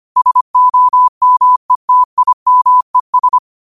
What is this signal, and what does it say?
Morse code, heard as a keyed tone, about 25 wpm, 1000 Hz. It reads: SOMETIMES
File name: qotdfiles / 25wpm